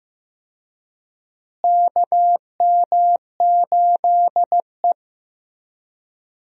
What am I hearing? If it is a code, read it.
Morse code, heard as a keyed tone, about 15 wpm, 700 Hz. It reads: KM8E
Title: USA Call Signs - 4 Characters - Repeated 2x - 15wpm